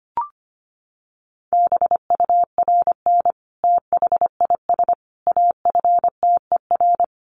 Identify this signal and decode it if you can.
Morse code, heard as a keyed tone, about 25 wpm, 700 Hz. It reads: BURN THIS AFTER
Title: Sentences from Top 1300 Words 25wpm